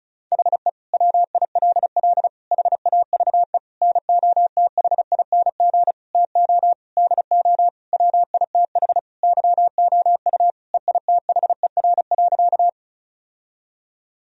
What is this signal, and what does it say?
Morse code, heard as a keyed tone, about 34 wpm, 700 Hz. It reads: HE WILL HAVE NOTHING TO DO WITH YOU EITHER.
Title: Sentences from Top 1000 Words - Review Part 4 - 35wpm